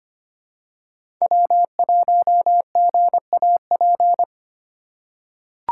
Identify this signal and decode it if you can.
Morse code, heard as a keyed tone, about 25 wpm, 700 Hz. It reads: W1GAP
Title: USA Call Signs - 2x Character Spacing - 25wpm